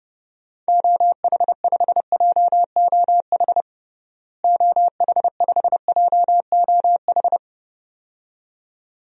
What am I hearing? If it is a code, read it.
Morse code, heard as a keyed tone, about 29 wpm, 700 Hz. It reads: OH5JOH OH5JOH
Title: DXCC-100 Call Signs - 6 Characters - Repeated 2x - 30wpm